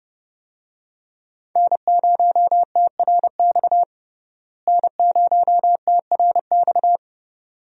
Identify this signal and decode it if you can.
Morse code, heard as a keyed tone, about 29 wpm, 700 Hz. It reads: N0TRX N0TRX
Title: USA Call Signs - 5 Characters - Repeated 2x - 30wpm